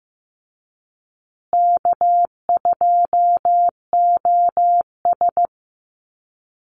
Morse code, heard as a keyed tone, about 15 wpm, 700 Hz. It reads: K2OS
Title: Easy Call Signs Repeated 2x - 15wpm